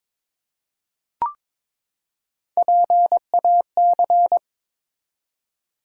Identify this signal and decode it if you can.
Morse code, heard as a keyed tone, about 22 wpm, 700 Hz. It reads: PAC
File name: ARRL Sections v2 - 22wpm